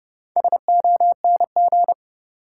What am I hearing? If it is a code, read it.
Morse code, heard as a keyed tone, about 32 wpm, 700 Hz. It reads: SONG
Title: Sentences from Top 400 Words - New Word Drill - 30wpm